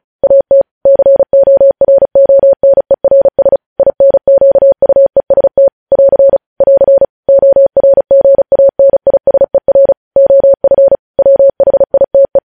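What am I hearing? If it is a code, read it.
Morse code, heard as a keyed tone, about 36 wpm, 550 Hz. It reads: AT CORONERS INQUEST <AR> <AR> ORGANISER OF WHITE
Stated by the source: Morse news headlines practice at 35 words per minute.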